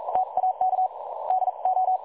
Morse code, heard as a keyed tone, about 42 wpm, spaced as for 35 wpm, 750 Hz. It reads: TAK DO
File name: dlazaawansowanych / nagraniazpasma / websdr recording 3565.0kHz